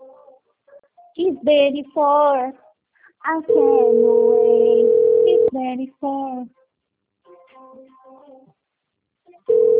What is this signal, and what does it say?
Ringback tone (North America), 440 + 480 Hz, with speech.
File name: out- - - - - .81874